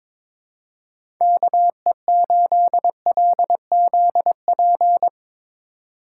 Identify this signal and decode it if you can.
Morse code, heard as a keyed tone, about 22 wpm, 700 Hz. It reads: KE8LZP
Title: USA Call Signs - 6 Characters - 4x Character Spacing - 22wpm